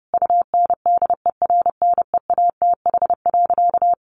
Morse code, heard as a keyed tone, about 30 wpm, 700 Hz. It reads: UNDERNEATH.
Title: A Sound Of Thunder 30wpm